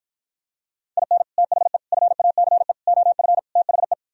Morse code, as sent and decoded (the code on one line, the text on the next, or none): .. -. / - .... / ..-. .- -.-. / --- ..-. / - .... .
IN THE FACE OF THE